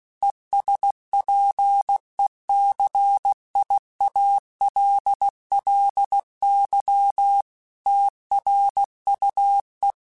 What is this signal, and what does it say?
Morse code, heard as a keyed tone, about 16 wpm, 800 Hz. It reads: ESPECIALLY TRUE
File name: Technical Subjects / Technical Subjects-mp3s / AND Gate @15